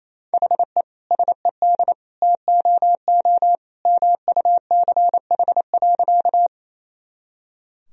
Morse code, heard as a keyed tone, about 28 wpm, 700 Hz. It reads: HE SED TOO MUCH.
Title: Sentences from Top 200 Words and Abbreviations 28wpm